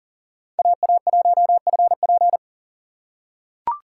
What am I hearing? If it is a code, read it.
Morse code, heard as a keyed tone, about 40 wpm, 700 Hz. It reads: AA1FP
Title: USA Call Signs - 5 Characters - 6x Character Spacing - 40wpm